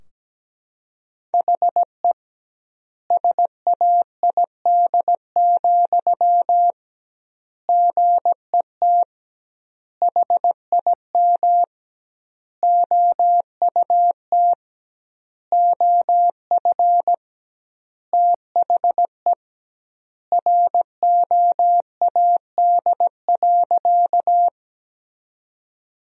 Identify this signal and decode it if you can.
Morse code, heard as a keyed tone, about 17 wpm, 700 Hz. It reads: HE SAID, GET HIM OUT OF THE ROAD.
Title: 2x Word-Spacing - Sentences from Top 400 Words - 17wpm